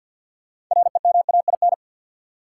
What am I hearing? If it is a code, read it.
Morse code, heard as a keyed tone, about 48 wpm, 700 Hz. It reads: REMAIN
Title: Sentences from Top 1200 Words - New Word Drill Alt - 50wpm